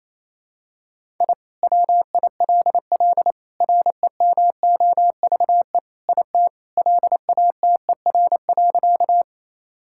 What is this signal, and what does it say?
Morse code, heard as a keyed tone, about 28 wpm, 700 Hz. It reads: I WILL REMOVE IT LATER.
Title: Sentences from Top 1000 Words - Review Part 5 - 28wpm